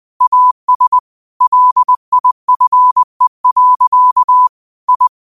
Morse code, heard as a keyed tone, about 20 wpm, 1000 Hz. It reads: AS LIFE. I
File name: qotdfiles / 20wpm